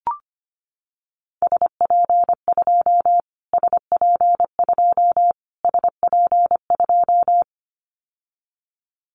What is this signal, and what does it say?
Morse code, heard as a keyed tone, about 25 wpm, 700 Hz. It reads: SP2 SP2 SP2
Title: Callsign Prefixes for Top 25 DXCC - Repeated 3x - 25wpm